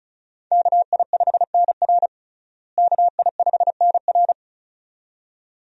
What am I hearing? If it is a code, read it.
Morse code, heard as a keyed tone, about 35 wpm, 700 Hz. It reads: KI5NR KI5NR
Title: USA Call Signs - 5 Characters - Repeated 2x - 35wpm